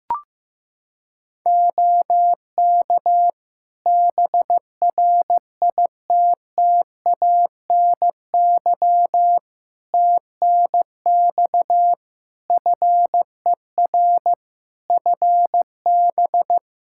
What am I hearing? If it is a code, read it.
Morse code, heard as a keyed tone, about 15 wpm, 700 Hz. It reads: OK BRITTANY TNX FER FB
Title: K1USN SST QSO Practice 15wpm